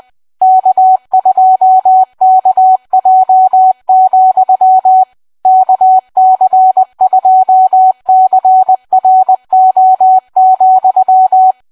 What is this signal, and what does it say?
Morse code, heard as a keyed tone, about 20 wpm, 750 Hz. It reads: K2KJ, KC2CRO,